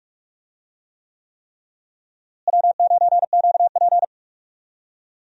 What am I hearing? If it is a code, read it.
Morse code, heard as a keyed tone, about 46 wpm, 700 Hz. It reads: W9QP
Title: USA Call Signs - 4 Characters - Repeated 3x - 45wpm